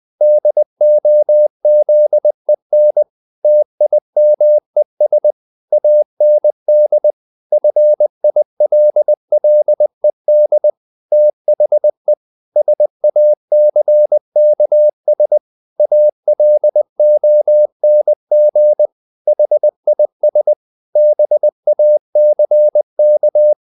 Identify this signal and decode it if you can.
Morse code, heard as a keyed tone, about 20 wpm, 600 Hz. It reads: DOZEN TIMES AND FILLED THE SACKS ALONG HIS BACK
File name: the old man and the sea part 2 20wpm